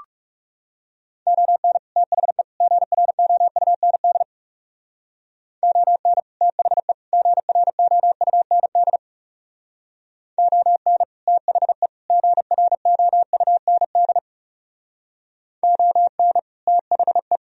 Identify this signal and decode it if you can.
Morse code, heard as a keyed tone, about 37 wpm, 700 Hz. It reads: ON THE GROUND ON THE GROUND ON THE GROUND ON THE
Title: Speed-Racing - Sets of 3 Words 30wpm